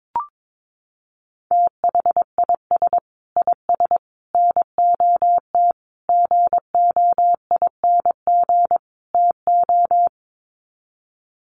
Morse code, heard as a keyed tone, about 22 wpm, 700 Hz. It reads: THIS IS NOT GOING TO
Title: Sets of 5 Words Encore2 22wpm